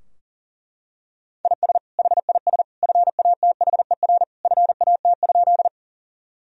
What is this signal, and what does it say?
Morse code, heard as a keyed tone, about 40 wpm, 700 Hz. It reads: IS HIS FATHER FAT?
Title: Sentences from Top 1000 Words - Review Part 4 - 40wpm